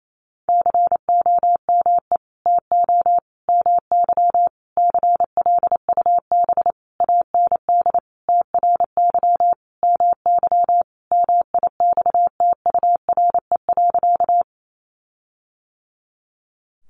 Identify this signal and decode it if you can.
Morse code, heard as a keyed tone, about 28 wpm, 700 Hz. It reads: COME TO MY CLUB AND TRY MY MIXTURE.
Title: Sentences from Top 1500 Words 28wpm